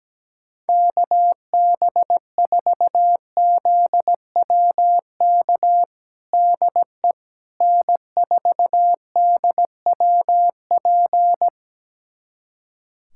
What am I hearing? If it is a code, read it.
Morse code, heard as a keyed tone, about 17 wpm, 700 Hz. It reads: KB4ZWK DE N4DWP
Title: USA Callsign DE Callsign 17wpm